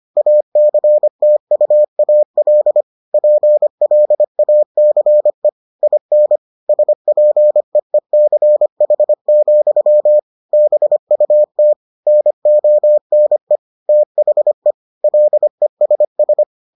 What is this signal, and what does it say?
Morse code, heard as a keyed tone, about 25 wpm, 600 Hz. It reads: ACTUAL PLACE IN SPEECH, BUT NONE THE LESS